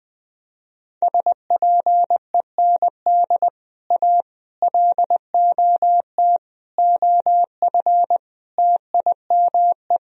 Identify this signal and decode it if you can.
Morse code, heard as a keyed tone, about 20 wpm, 700 Hz. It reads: SPEND A LOT OF TIME
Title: Speed-Racing - Sets of 5 Words 17wpm